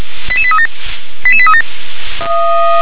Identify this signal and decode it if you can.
Five-tone selective call (ZVEI), heard as quick runs of five tones, tones of about 70 ms. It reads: 88837 88837
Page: SDR/ZVEI